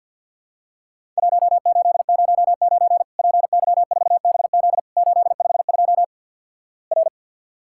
Morse code, heard as a keyed tone, about 50 wpm, 700 Hz. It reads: 1809 PY4BZ 852 R
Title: Worked All Europe DX Contest - QTC Exchange - 50wpm